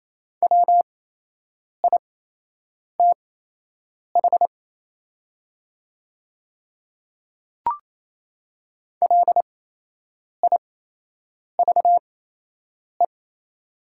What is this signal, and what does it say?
Morse code, heard as a keyed tone, about 28 wpm, spaced as for 8 wpm, 700 Hz. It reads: WITH LIVE
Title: 8x Character Spacing - Top 200 Words - 28wpm